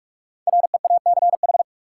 Morse code, heard as a keyed tone, about 46 wpm, 700 Hz. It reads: REACH
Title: Five Letter Words - 45wpm